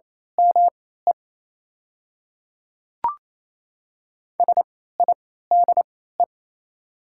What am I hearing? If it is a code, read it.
Morse code, heard as a keyed tone, about 27 wpm, spaced as for 16 wpm, 700 Hz. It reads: ME SIDE
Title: 3x Character Spacing - Top 500 Words - 28wpm